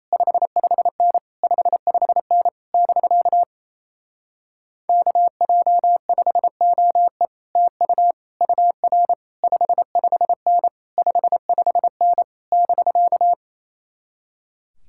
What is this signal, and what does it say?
Morse code, heard as a keyed tone, about 27 wpm, 700 Hz. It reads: 55N 55N <BK> KJ5OE TU UR 55N 55N <BK>
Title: Speed-Racing Common QSO Phrases - 28wpm